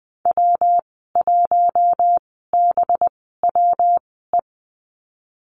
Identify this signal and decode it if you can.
Morse code, heard as a keyed tone, about 19 wpm, spaced as for 14 wpm, 700 Hz. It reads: W1BWE
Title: 186 Lesson 30g - Letter b - Call Signs 20wpm